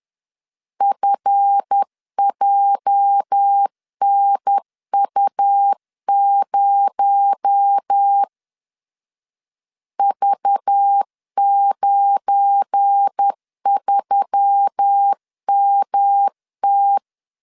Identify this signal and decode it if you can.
Morse code, heard as a keyed tone, about 10 wpm, 800 Hz. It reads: FJNU0 V93MT
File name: LEZIONI / 8 YJ45 / 8 YJ 10wpm 2651